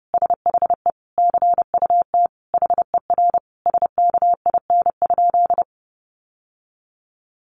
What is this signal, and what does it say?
Morse code, heard as a keyed tone, about 30 wpm, 700 Hz. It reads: SHE CUT HER SKIN?
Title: Sentences from Top 1300 Words 30wpm